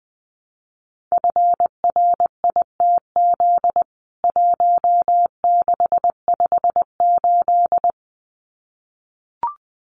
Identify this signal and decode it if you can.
Morse code, heard as a keyed tone, about 20 wpm, 700 Hz. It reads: FRITZ 1658
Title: CWT Exchange 20wpm